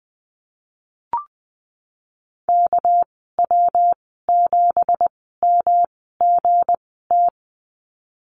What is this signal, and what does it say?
Morse code, heard as a keyed tone, about 20 wpm, spaced as for 15 wpm, 700 Hz. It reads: KW7MGT
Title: 231 Lesson 37g - Letter x - Call Signs 20wpm